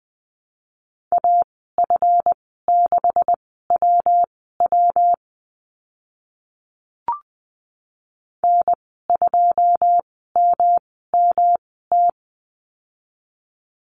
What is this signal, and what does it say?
Morse code, heard as a keyed tone, about 20 wpm, spaced as for 15 wpm, 700 Hz. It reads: AF6WW N2MMT
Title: 140 Lesson 23g - Letter f - Call Signs 20wpm